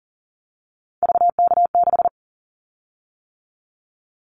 Morse code, heard as a keyed tone, about 40 wpm, 700 Hz. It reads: VK6
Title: Callsign Prefixes for Top 50 DXCC - 40wpm